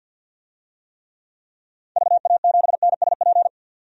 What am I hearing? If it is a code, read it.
Morse code, heard as a keyed tone, about 46 wpm, 700 Hz. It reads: VA7NSP
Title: DXCC-100 Call Signs - 6 Characters - 50wpm